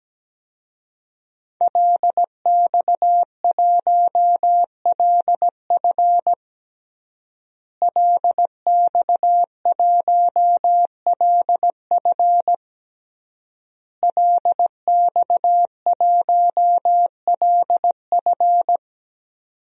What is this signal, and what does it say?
Morse code, heard as a keyed tone, about 17 wpm, 700 Hz. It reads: LX1LF LX1LF LX1LF
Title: DXCC-100 Call Signs - 5 Characters - Repeated 3x - 17wpm